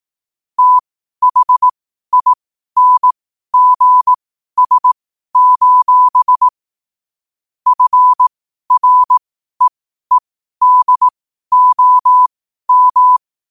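Morse code, heard as a keyed tone, about 18 wpm, spaced as for 12 wpm, 1000 Hz. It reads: THINGS: FREEDOM
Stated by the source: 12 wpm Morse code.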